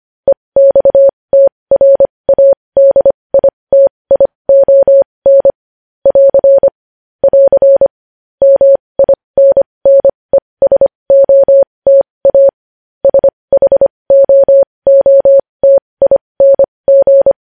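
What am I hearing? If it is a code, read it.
Morse code, heard as a keyed tone, about 25 wpm, spaced as for 20 wpm, 550 Hz. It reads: EXTRADITION <AR> <AR> MINNESOTA SHOOTING
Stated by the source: Morse code news headlines practice at 20 wpm.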